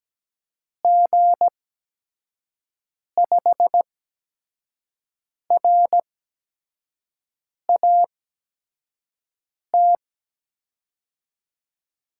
Morse code, heard as a keyed tone, about 17 wpm, spaced as for 5 wpm, 700 Hz. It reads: G5RAT
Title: USA Call Signs - 6 Characters - 8x Character Spacing - 17wpm